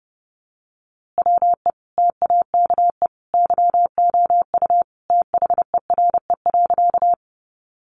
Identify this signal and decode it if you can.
Morse code, heard as a keyed tone, about 30 wpm, 700 Hz. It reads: WE TAKE YOU THERE.
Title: A Sound Of Thunder 30wpm